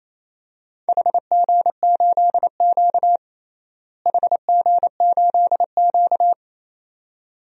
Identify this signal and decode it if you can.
Morse code, heard as a keyed tone, about 28 wpm, 700 Hz. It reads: HG8Q HG8Q
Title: DXCC-100 Call Signs - 4 Characters - Repeated 2x - 28wpm